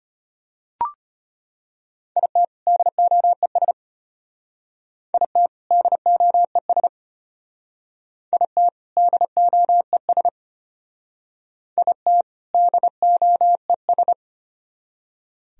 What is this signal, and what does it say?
Morse code, heard as a keyed tone, about 34 wpm, 700 Hz. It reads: IT DOES IT DOES IT DOES IT DOES